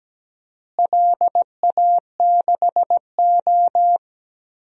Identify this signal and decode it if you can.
Morse code, heard as a keyed tone, about 17 wpm, 700 Hz. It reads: LA6O
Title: DXCC-100 Call Signs - 4 Characters - Repeated 3x - 17wpm